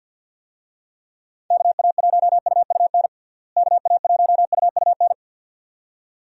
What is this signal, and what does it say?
Morse code, heard as a keyed tone, about 50 wpm, 700 Hz. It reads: KA1UUN KA1UUN
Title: USA Call Signs - 6 Characters - Repeated 2x - 50wpm